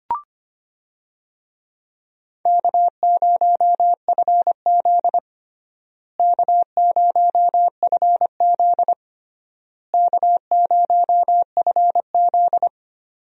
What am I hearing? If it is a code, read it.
Morse code, heard as a keyed tone, about 25 wpm, 700 Hz. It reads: K0FZ K0FZ K0FZ
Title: USA Call Signs - 4 Characters - Repeated 3x - 25wpm